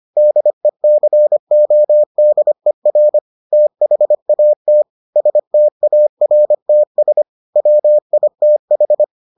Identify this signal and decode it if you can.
Morse code, heard as a keyed tone, about 25 wpm, 600 Hz. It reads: DECODER THAT STARTS WITH